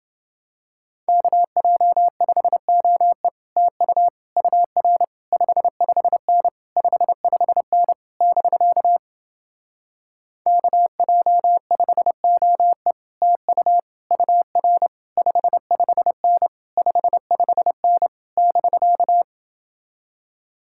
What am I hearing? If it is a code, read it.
Morse code, heard as a keyed tone, about 30 wpm, 700 Hz. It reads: KJ5OE TU UR 55N 55N <BK> KJ5OE TU UR 55N 55N <BK>